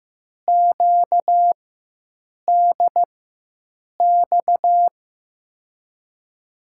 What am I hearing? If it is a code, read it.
Morse code, heard as a keyed tone, about 15 wpm, spaced as for 7 wpm, 700 Hz. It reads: QDX
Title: USA Call Signs - 6 Characters - 4x Character Spacing - 15wpm